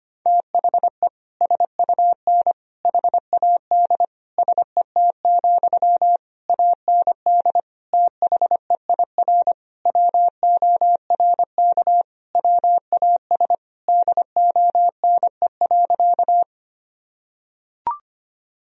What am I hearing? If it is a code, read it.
Morse code, heard as a keyed tone, about 25 wpm, 700 Hz. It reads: THE SUN HAD SET, AND THEIR WORK WAS DONE.
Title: Sentences from Top 1000 Words - Review Part 1 - 25wpm